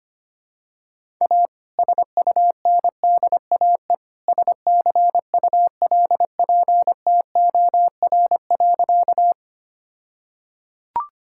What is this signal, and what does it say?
Morse code, heard as a keyed tone, about 25 wpm, 700 Hz. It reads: A SUNDAE SCULPTOR.
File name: practice / 25wpm / All-Age Appropriate Jokes - 25wpm